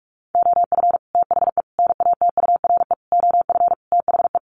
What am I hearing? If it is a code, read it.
Morse code, heard as a keyed tone, about 44 wpm, 700 Hz. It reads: OF THE NATURE OF THE
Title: Sets of 5 Words Encore2 45wpm